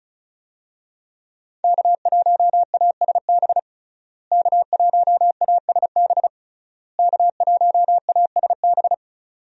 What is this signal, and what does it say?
Morse code, heard as a keyed tone, about 36 wpm, 700 Hz. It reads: K1ASB K1ASB K1ASB
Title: USA Call Signs - 5 Characters - Repeated 3x - 35wpm